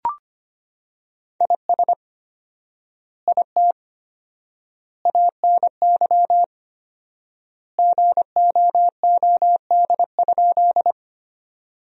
Morse code, heard as a keyed tone, about 25 wpm, 700 Hz. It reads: IS IT ANY GOOD?
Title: 4x Word-Spacing - Sentences from Top 200 Words - 25wpm